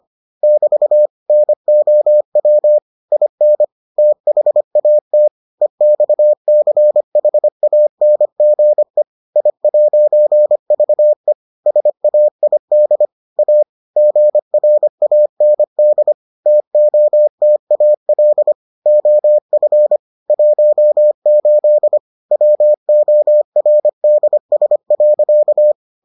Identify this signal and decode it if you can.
Morse code, heard as a keyed tone, about 25 wpm, 600 Hz. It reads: <BT> NOW IN THAT EXCHANGE I'VE SAID A GRAND TOTAL OF 18 WORDS.